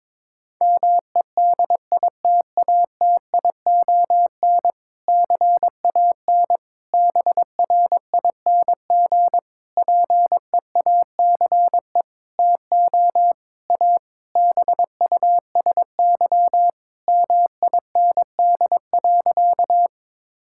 Morse code, heard as a keyed tone, about 22 wpm, 700 Hz. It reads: MEDITATION CAN BRING PEACE TO A BUSY MIND.
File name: Difficult Sentences with Hints - 22wpm